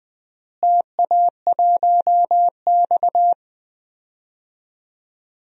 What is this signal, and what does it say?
Morse code, heard as a keyed tone, about 20 wpm, 700 Hz. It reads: TA1X
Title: DXCC-100 Call Signs - 4 Characters - Repeated 3x - 20wpm